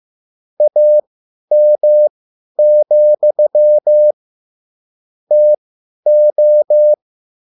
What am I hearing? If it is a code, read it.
Morse code, heard as a keyed tone, about 15 wpm, spaced as for 10 wpm, 600 Hz. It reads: AM, TO
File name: QSL / CW lesson